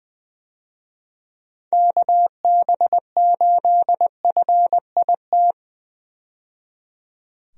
Morse code, heard as a keyed tone, about 20 wpm, 700 Hz. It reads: KB8FIT
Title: USA Call Signs - 6 Characters - 20wpm